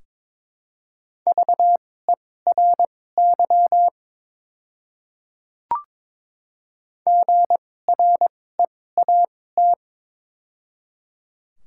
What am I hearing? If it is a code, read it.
Morse code, heard as a keyed tone, about 22 wpm, spaced as for 16 wpm, 700 Hz. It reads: VERY GREAT
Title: 2x Character Spacing - Top 200 Words - 22wpm